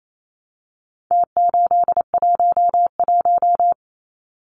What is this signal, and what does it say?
Morse code, heard as a keyed tone, about 28 wpm, 700 Hz. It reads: T811
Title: Numbers - Contest Style - 4-Digit 28wpm